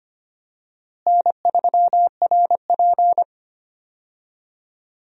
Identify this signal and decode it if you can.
Morse code, heard as a keyed tone, about 25 wpm, 700 Hz. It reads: N3RP
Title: USA Call Signs - 4 Characters - Repeated 2x - 25wpm